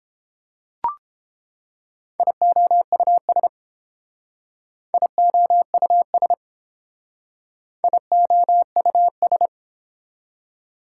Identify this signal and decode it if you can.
Morse code, heard as a keyed tone, about 28 wpm, 700 Hz. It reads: IOUS IOUS IOUS